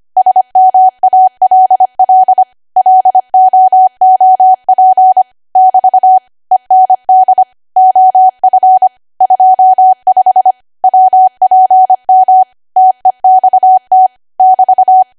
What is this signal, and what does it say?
Morse code, heard as a keyed tone, about 25 wpm, 750 Hz. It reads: SMALL LOOP <BT> END OF 25 WPM TEXT <BT>
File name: Archive / 25 WPM / 25WPM